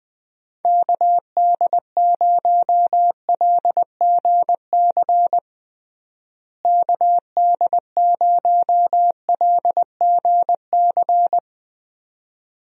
Morse code, heard as a keyed tone, about 20 wpm, 700 Hz. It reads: KD0LGC KD0LGC
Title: USA Call Signs - 6 Characters - Repeated 2x - 20wpm